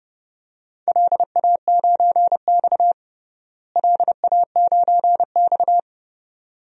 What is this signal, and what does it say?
Morse code, heard as a keyed tone, about 30 wpm, 700 Hz. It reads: LA9X LA9X
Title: DXCC-100 Call Signs - 4 Characters - Repeated 2x - 30wpm